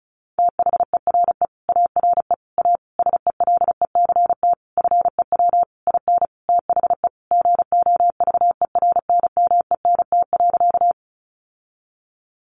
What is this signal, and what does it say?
Morse code, heard as a keyed tone, about 34 wpm, 700 Hz. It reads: THERE ARE A SELECT FEW IN THE GOVERNMENT.
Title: Sentences from Top 1300 Words 35wpm